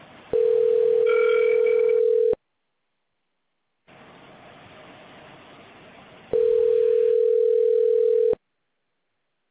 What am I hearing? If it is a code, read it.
Ringback tone (North America), 440 + 480 Hz.